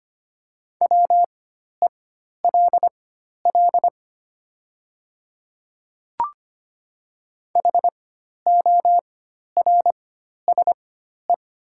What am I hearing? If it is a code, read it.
Morse code, heard as a keyed tone, about 25 wpm, spaced as for 12 wpm, 700 Hz. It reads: WELL HORSE
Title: 4x Character Spacing - Top 300 Words - 25wpm